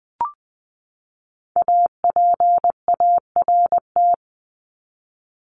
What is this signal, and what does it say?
Morse code, heard as a keyed tone, about 20 wpm, 700 Hz. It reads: APART
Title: Top 1501-1750 Words 20wpm